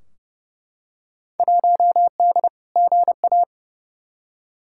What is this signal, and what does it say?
Morse code, heard as a keyed tone, about 32 wpm, 700 Hz. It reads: 1D GA
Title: Field Day Exchanges - 30wpm